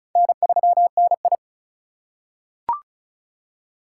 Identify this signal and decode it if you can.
Morse code, heard as a keyed tone, about 36 wpm, 700 Hz. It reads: N3NI
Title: USA Call Signs - 2x Character Spacing - 35wpm